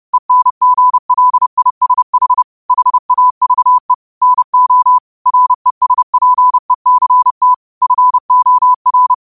Morse code, heard as a keyed tone, about 30 wpm, 1000 Hz. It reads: ENGLISH HAVE NO RESPECT FOR
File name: qotdfiles / 30wpm